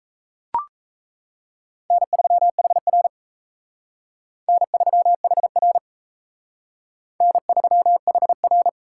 Morse code, heard as a keyed tone, about 34 wpm, 700 Hz. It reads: N3HR N3HR N3HR
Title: Speed Racing - USA 1x2 Call Signs 28wpm